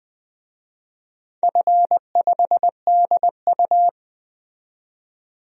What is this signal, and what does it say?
Morse code, heard as a keyed tone, about 20 wpm, 700 Hz. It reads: F5DU
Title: DXCC-100 Call Signs - 4 Characters - 20wpm